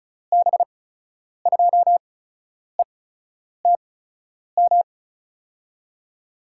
Morse code, heard as a keyed tone, about 36 wpm, spaced as for 10 wpm, 700 Hz. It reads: B2ETM
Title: USA Call Signs - 6 Characters - 8x Character Spacing - 35wpm